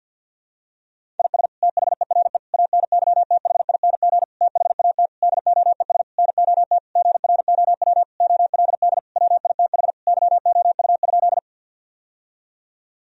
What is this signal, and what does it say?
Morse code, heard as a keyed tone, about 53 wpm, 700 Hz. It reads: IS THERE ANYTHING THAT DOES NOT GROW OLD WITH YOU?
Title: Sentences from Top 1000 Words - Review Part 1 - 50wpm